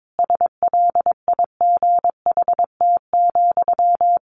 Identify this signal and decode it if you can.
Morse code, heard as a keyed tone, about 22 wpm, 700 Hz. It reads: SLIGHT,
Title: A Sound Of Thunder 22wpm